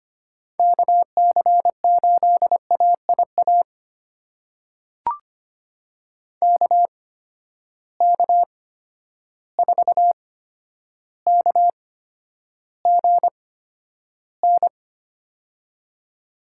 Morse code, heard as a keyed tone, about 25 wpm, spaced as for 11 wpm, 700 Hz. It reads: KC8AIA KK4KGN
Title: USA Call Signs - 6 Characters - 8x Character Spacing - 25wpm